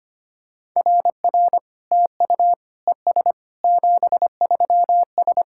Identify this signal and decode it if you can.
Morse code, heard as a keyed tone, about 25 wpm, 700 Hz. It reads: RR TU ES 73S
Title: SOTA QSO Practice 25wpm